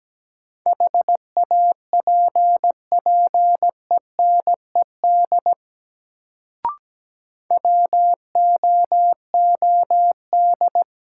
Morse code, heard as a keyed tone, about 17 wpm, 700 Hz. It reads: HAPPENED WOOD
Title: Sentences from Top 1000 Words - Review Part 2 - New Word Drill - 17wpm